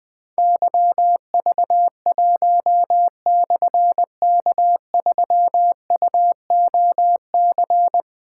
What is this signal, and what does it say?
Morse code, heard as a keyed tone, about 20 wpm, 700 Hz. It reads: YV1/K3UOC
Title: Difficult Call Signs 20wpm